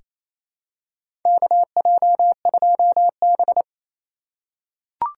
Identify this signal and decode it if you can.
Morse code, heard as a keyed tone, about 28 wpm, 700 Hz. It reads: KJ2B
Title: USA Call Signs - 4 Characters - 8x Character Spacing - 28wpm